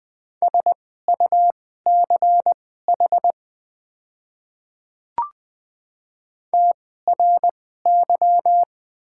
Morse code, heard as a keyed tone, about 20 wpm, spaced as for 14 wpm, 700 Hz. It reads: SUCH TRY
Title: 2x Character Spacing - Top 200 Words - 20wpm